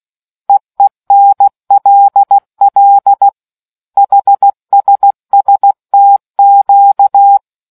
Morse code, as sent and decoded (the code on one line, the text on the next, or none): . -. .-.. .-.. / .... ... ... - --.-
EENLL HSSTQ